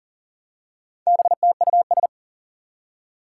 Morse code, heard as a keyed tone, about 42 wpm, 700 Hz. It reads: BTUS
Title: Four Challenging Letter-Number Combinations 40wpm